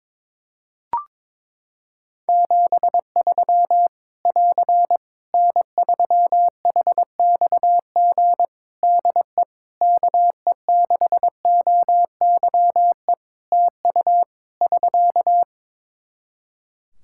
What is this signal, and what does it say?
Morse code, heard as a keyed tone, about 22 wpm, 700 Hz. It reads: 73 <AR> N3HXG DE KE6OYE TU <SK>
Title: Practice QSO - Exchange 4 - Ending - 22wpm